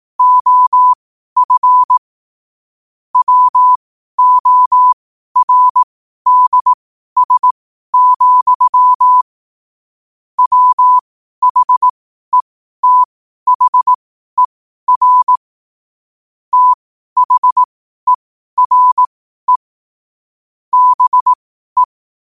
Morse code, heard as a keyed tone, about 18 wpm, spaced as for 12 wpm, 1000 Hz. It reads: OF WORDS, WHETHER THERE BE
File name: qotdfiles / 12wpm